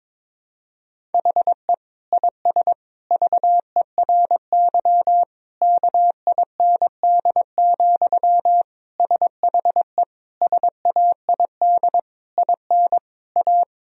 Morse code, heard as a keyed tone, about 22 wpm, 700 Hz. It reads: HE IS VERY KIND, SHE SAID IN A
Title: Sentences from Top 1000 Words - Review Part 1 - 22wpm